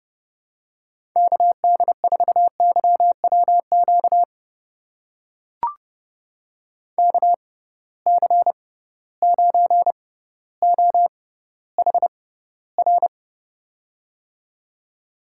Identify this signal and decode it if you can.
Morse code, heard as a keyed tone, about 30 wpm, spaced as for 15 wpm, 700 Hz. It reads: KD4YWQ KC9OHR
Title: USA Call Signs - 6 Characters - 6x Character Spacing - 30wpm